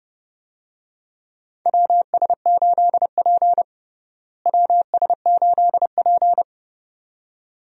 Morse code, heard as a keyed tone, about 30 wpm, 700 Hz. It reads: WS8P WS8P
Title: USA Call Signs - 4 Characters - Repeated 2x - 30wpm